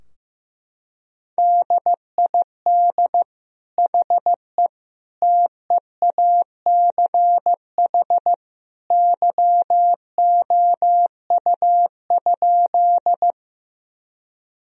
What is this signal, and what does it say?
Morse code, heard as a keyed tone, about 15 wpm, 700 Hz. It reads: DID HE TEACH YOU?
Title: Sentences from Top 1000 Words - Review Part 2 - 15wpm